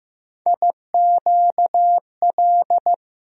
Morse code, heard as a keyed tone, about 15 wpm, 700 Hz. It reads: IQL
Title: USA Call Sign Postfixes - 3 Letters 15wpm